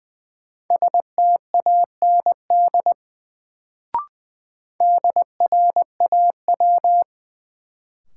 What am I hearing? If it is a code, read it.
Morse code, heard as a keyed tone, about 20 wpm, 700 Hz. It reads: STAND DRAW
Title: Sentences from Top 300 Words - New Word Drill - 20wpm